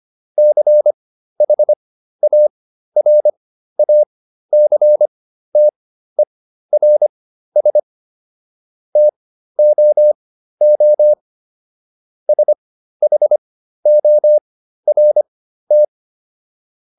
Morse code, heard as a keyed tone, about 25 wpm, spaced as for 13 wpm, 600 Hz. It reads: CHARACTERS TOO SHORT